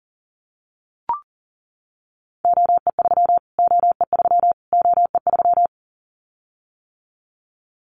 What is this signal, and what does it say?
Morse code, heard as a keyed tone, about 40 wpm, 700 Hz. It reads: OE3 OE3 OE3
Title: Callsign Prefixes for Top 25 DXCC - Repeated 3x - 40wpm